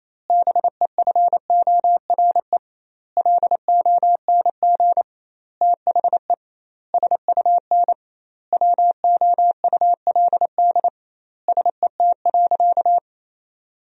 Morse code, heard as a keyed tone, about 27 wpm, 700 Hz. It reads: BEFORE LONG THE SUN WOULD SET.
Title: 2x Word-Spacing - Sentences from Top 400 Words - 28wpm